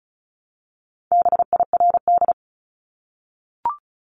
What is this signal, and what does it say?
Morse code, heard as a keyed tone, about 34 wpm, 700 Hz. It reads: BIRD
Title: Top 500 Words Repeated 35wpm